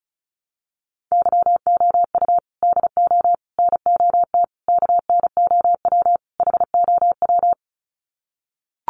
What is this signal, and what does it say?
Morse code, heard as a keyed tone, about 34 wpm, 700 Hz. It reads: YOU DO NOT KNOW HOW
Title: Sets of 5 Words Encore 35wpm